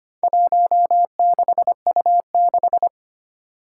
Morse code, heard as a keyed tone, about 25 wpm, 700 Hz. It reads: 16U6
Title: Four Challenging Letter-Number Combinations 25wpm